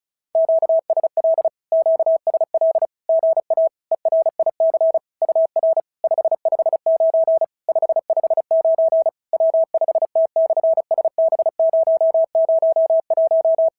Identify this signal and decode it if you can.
Morse code, heard as a keyed tone, about 34 wpm, 650 Hz. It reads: QSL QSL GA ERIC UR 559 559 W5T/SB001
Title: SOTA QSO Practice 35wpm